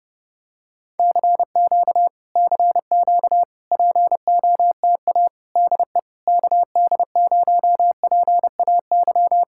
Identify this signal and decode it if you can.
Morse code, heard as a keyed tone, about 29 wpm, 700 Hz. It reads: CQ CQ POTA DE KD0PAY